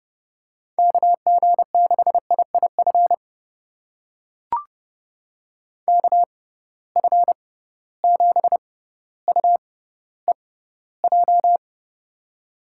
Morse code, heard as a keyed tone, about 30 wpm, spaced as for 15 wpm, 700 Hz. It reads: KG6IIF KF7UEJ
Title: USA Call Signs - 6 Characters - 6x Character Spacing - 30wpm